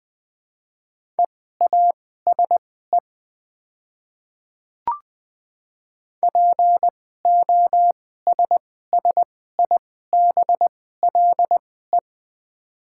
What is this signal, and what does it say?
Morse code, heard as a keyed tone, about 20 wpm, spaced as for 14 wpm, 690 Hz. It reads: EASE POSSIBLE
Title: 2x Character Spacing - Top 500 Words - 20wpm